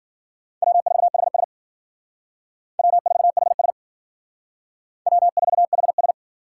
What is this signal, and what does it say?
Morse code, heard as a keyed tone, about 51 wpm, 700 Hz. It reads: W4HS W4HS W4HS